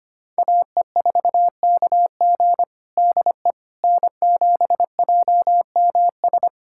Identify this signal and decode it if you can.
Morse code, heard as a keyed tone, about 25 wpm, 700 Hz. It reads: AE4KG DE N7JMS
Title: USA Callsign DE Callsign - 5 Characters - 25wpm